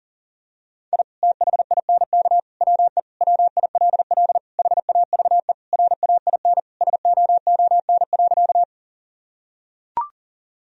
Morse code, heard as a keyed tone, about 40 wpm, 700 Hz. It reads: I THINK WE WILL HAVE RAIN SOON.
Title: Sentences from Top 1000 Words - Review Part 1 - 40wpm